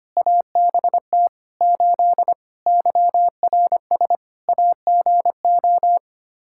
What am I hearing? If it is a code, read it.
Morse code, heard as a keyed tone, about 25 wpm, 700 Hz. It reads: ABT 8 YRS AGO
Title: Practice QSO - Exchange 3 - More Info - 25wpm